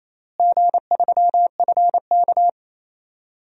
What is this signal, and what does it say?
Morse code, heard as a keyed tone, about 28 wpm, 700 Hz. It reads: G3FK
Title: DXCC-100 Call Signs - 4 Characters - Repeated 2x - 28wpm